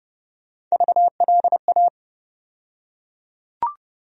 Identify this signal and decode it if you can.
Morse code, heard as a keyed tone, about 31 wpm, 700 Hz. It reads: VLA
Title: USA Call Sign Postfixes - 3 Letters 30wpm